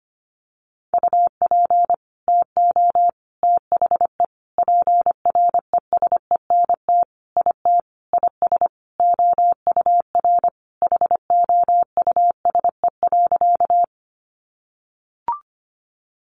Morse code, heard as a keyed tone, about 25 wpm, 700 Hz. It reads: UP TO THE PRESENT IT IS OUR HOUSE.
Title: Sentences from Top 800 Words 25wpm